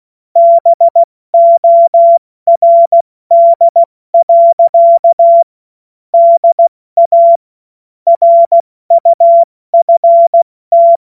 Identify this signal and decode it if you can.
Morse code, heard as a keyed tone, about 16 wpm, 680 Hz. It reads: BORD. DA RUFT
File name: Morsekurs / Witze / Witz 03. Mai 2022 70 F80